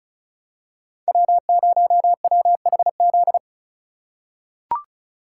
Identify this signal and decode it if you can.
Morse code, heard as a keyed tone, about 36 wpm, 700 Hz. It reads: W0WHZ
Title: USA Call Signs - 3x Character Spacing - 35wpm